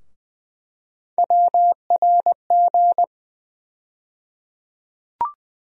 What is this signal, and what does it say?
Morse code, heard as a keyed tone, about 21 wpm, 700 Hz. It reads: WRG
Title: USA Call Sign Postfixes - 3 Letters 20wpm